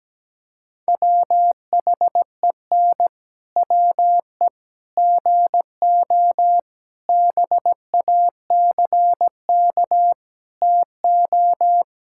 Morse code, heard as a keyed tone, about 17 wpm, 700 Hz. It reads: WHEN WE GO BACK TO
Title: Pangram Challenge v2 - 17wpm